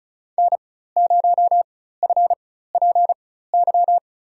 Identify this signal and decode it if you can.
Morse code, heard as a keyed tone, about 36 wpm, spaced as for 16 wpm, 700 Hz. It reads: N0FPY
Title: USA Call Signs - 5 Characters - 4x Character Spacing - 35wpm